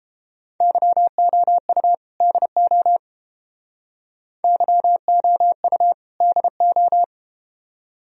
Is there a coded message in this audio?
Yes: Morse code from 0.6 to 7.0 s, heard as a keyed tone, about 31 wpm, 700 Hz. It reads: YOU DO YOU DO